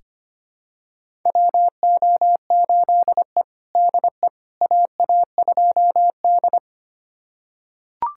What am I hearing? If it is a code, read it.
Morse code, heard as a keyed tone, about 25 wpm, 700 Hz. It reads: WO8E DE AA2D
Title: USA Callsign DE Callsign - 4 Characters - 25wpm